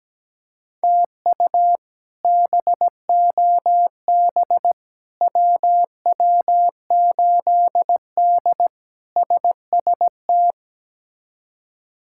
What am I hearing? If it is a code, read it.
Morse code, heard as a keyed tone, about 17 wpm, 700 Hz. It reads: TU BOB WW8D SST
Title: K1USN SST QSO Practice - 17wpm